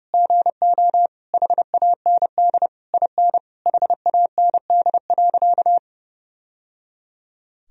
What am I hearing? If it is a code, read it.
Morse code, heard as a keyed tone, about 30 wpm, 700 Hz. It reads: GO HAND IN HAND.
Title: Sentences from Top 1000 Words - Review Part 4 - 30wpm